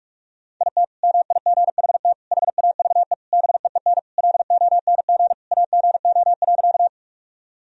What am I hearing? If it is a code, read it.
Morse code, heard as a keyed tone, about 46 wpm, 700 Hz. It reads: IT MIGHT HAVE BEEN LONG AGO.
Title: Sentences from Top 1000 Words - Review Part 3 - 45wpm